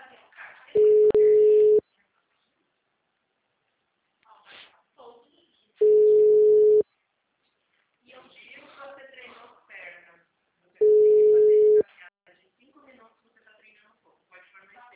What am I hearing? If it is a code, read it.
Ringback tone (Europe, 425 Hz), with speech.
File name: out- -2004- - - .2